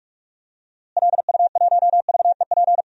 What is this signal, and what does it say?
Morse code, heard as a keyed tone, about 43 wpm, 700 Hz. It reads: LU1VEP
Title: DXCC-100 Call Signs - 6 Characters - Repeated 2x - 45wpm